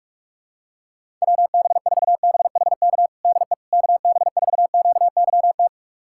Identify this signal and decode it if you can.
Morse code, heard as a keyed tone, about 47 wpm, 700 Hz. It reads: WB4BHK DE KB4QYT
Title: USA Callsign DE Callsign - 6 Characters - 45wpm